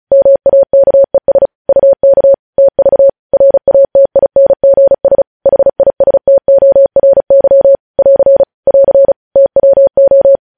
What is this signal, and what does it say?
Morse code, heard as a keyed tone, about 36 wpm, 550 Hz. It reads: MAKES UK TV RATINGS HISTORY <AR> <AR> TWO
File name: podcast / News headlines-1x-35x35